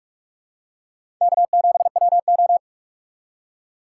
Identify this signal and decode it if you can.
Morse code, heard as a keyed tone, about 46 wpm, 700 Hz. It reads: K7WO
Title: USA Call Signs - 4 Characters - 3x Character Spacing - 45wpm